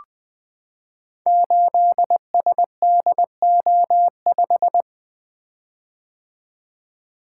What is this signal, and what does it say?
Morse code, heard as a keyed tone, about 20 wpm, 700 Hz. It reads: 8SDO5
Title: Code Groups - 5 Letter:Numbers - 20wpm